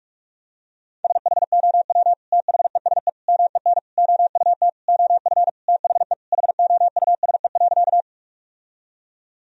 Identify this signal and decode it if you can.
Morse code, heard as a keyed tone, about 46 wpm, 700 Hz. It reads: SHOW THESE MEN OUT OF THE HOUSE.
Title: Sentences from Top 1000 Words - Review Part 1 - 45wpm